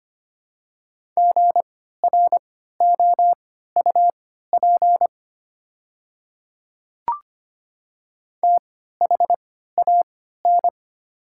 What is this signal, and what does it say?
Morse code, heard as a keyed tone, about 25 wpm, spaced as for 14 wpm, 690 Hz. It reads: GROUP THAN
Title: 3x Character Spacing - Top 500 Words - 25wpm